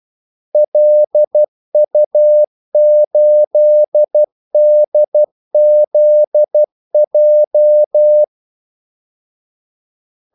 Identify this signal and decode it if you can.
Morse code, heard as a keyed tone, about 12 wpm, 600 Hz. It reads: LU8DZJ
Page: 2 ekstra ordmellemrum mellem kaldesignalerne